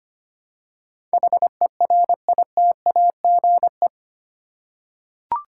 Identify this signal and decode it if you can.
Morse code, heard as a keyed tone, about 25 wpm, 700 Hz. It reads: HERITAGE
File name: 8 Letter Words - First 3 Letters Given - 25wpm